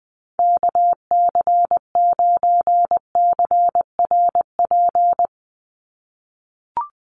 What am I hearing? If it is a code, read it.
Morse code, heard as a keyed tone, about 20 wpm, 700 Hz. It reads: KC9CRP
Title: USA Call Signs 20wpm